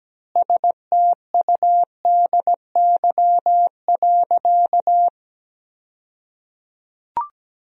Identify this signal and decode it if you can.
Morse code, heard as a keyed tone, about 17 wpm, 700 Hz. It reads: STUDY.
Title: 2x Word-Spacing - Sentences from Top 300 Words - 17wpm